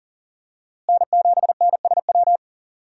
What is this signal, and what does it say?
Morse code, heard as a keyed tone, about 40 wpm, 700 Hz. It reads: N7NSW
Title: USA Call Signs - 5 Characters - Repeated 2x - 40wpm